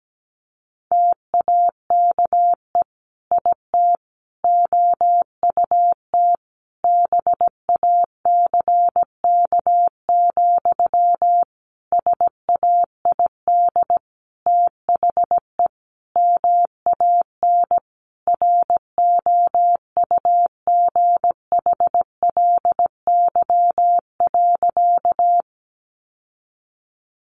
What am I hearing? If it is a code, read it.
Morse code, heard as a keyed tone, about 17 wpm, 700 Hz. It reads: TAKE IT OUT BACK, SAID THE MAN ROUGHLY.
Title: Sentences from Top 1500 Words 17wpm